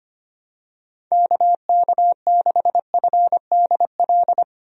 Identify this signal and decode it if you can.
Morse code, heard as a keyed tone, about 25 wpm, 700 Hz. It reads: KK6FDL
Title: USA Call Signs - 6 Characters - Repeated 3x - 25wpm